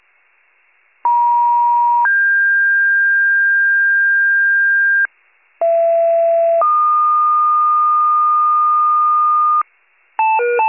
Two-tone page, heard as a one-second tone, then a three-second tone, about 950 then 1600 Hz, about 670 then 1220 Hz, with noise. It is heard from 1.0 to 9.6 s.